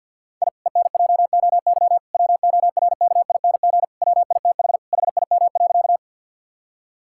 Morse code, heard as a keyed tone, about 50 wpm, 700 Hz. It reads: I ENJOY WORKING WITH HIM.
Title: Sentences from Top 1000 Words - Review Part 5 - 50wpm